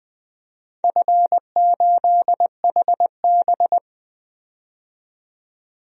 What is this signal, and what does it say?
Morse code, heard as a keyed tone, about 20 wpm, 700 Hz. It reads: F8HB
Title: DXCC-100 Call Signs - 4 Characters - 20wpm